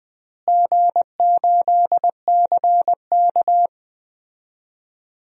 Morse code, heard as a keyed tone, about 20 wpm, 700 Hz. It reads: G8CK
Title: DXCC-100 Call Signs - 4 Characters - Repeated 3x - 20wpm